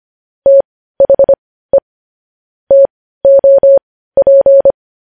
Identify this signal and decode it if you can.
Morse code, heard as a keyed tone, about 25 wpm, spaced as for 15 wpm, 550 Hz. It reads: THE TOP
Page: News Headlines in Morse Code at 15 WPM